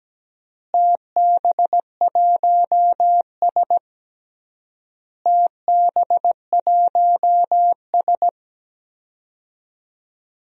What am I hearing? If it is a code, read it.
Morse code, heard as a keyed tone, about 17 wpm, 700 Hz. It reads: TB1S TB1S
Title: DXCC-100 Call Signs - 4 Characters - Repeated 3x - 17wpm